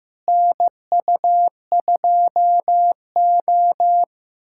Morse code, heard as a keyed tone, about 15 wpm, 700 Hz. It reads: NU2O
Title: USA Callsign DE Callsign - 4 Characters - 15wpm